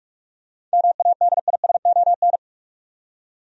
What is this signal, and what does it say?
Morse code, heard as a keyed tone, about 46 wpm, 700 Hz. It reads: MADISON
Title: USA Common Names - 45wpm